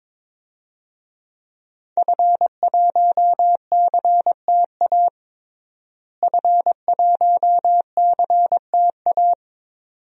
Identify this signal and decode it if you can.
Morse code, heard as a keyed tone, about 22 wpm, 700 Hz. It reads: F1CTA F1CTA
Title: DXCC-100 Call Signs - 5 Characters - Repeated 2x - 22wpm